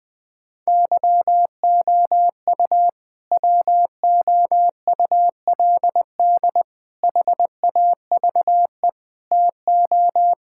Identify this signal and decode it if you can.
Morse code, heard as a keyed tone, about 21 wpm, 700 Hz. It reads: YOU WOULD HAVE TO
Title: Sentences with 3-Word Phrases - Spoken Beginning - 20wpm